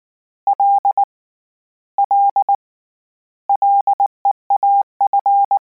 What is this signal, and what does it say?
Morse code, heard as a keyed tone, about 19 wpm, 800 Hz. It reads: L L LEAF
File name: Combinations / Combinations-mp3s / Combinations@18